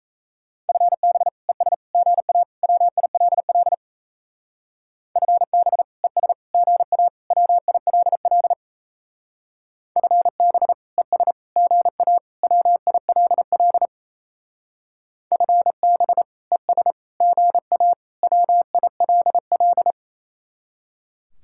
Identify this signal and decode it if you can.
Morse code, heard as a keyed tone, about 35 wpm, 700 Hz. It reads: FB ES GA WILL FB ES GA WILL FB ES GA WILL FB ES GA WILL